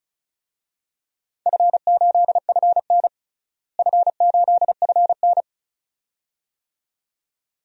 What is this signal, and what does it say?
Morse code, heard as a keyed tone, about 34 wpm, 700 Hz. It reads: F8FN F8FN
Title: DXCC-100 Call Signs - 4 Characters - Repeated 2x - 35wpm